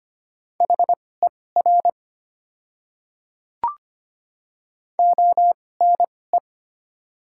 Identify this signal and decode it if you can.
Morse code, heard as a keyed tone, about 25 wpm, spaced as for 18 wpm, 700 Hz. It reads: HER ONE
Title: 2x Character Spacing - Top 200 Words - 25wpm